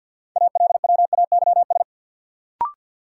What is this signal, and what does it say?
Morse code, heard as a keyed tone, about 53 wpm, 700 Hz. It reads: ALWAYS
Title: Sentences from Top 300 Words - New Word Drill - 50wpm